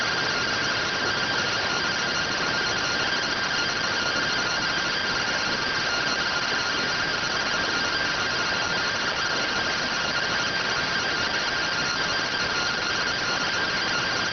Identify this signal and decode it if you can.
Two-tone teleprinter signal, about 170 Hz shift.